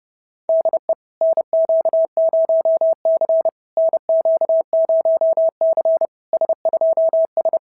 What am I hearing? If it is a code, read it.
Morse code, heard as a keyed tone, about 30 wpm, 690 Hz. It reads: DE NQ0C NQ0C S2S
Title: SOTA QSO Practice 30wpm